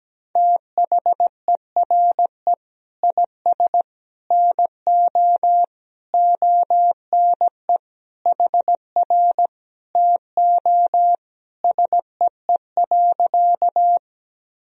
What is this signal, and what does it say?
Morse code, heard as a keyed tone, about 17 wpm, 700 Hz. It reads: THERE IS NO ONE HR TO SEE.
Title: Sentences from Top 200 Words and Abbreviations 17wpm